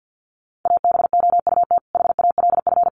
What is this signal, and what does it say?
Morse code, heard as a keyed tone, about 50 wpm, 700 Hz. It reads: ABOUT HALF
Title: Sentences from Top 600 Words 50wpm